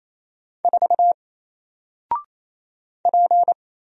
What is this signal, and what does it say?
Morse code, heard as a keyed tone, about 28 wpm, 700 Hz. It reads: 4 P
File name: Single-Letter-Number-28wpm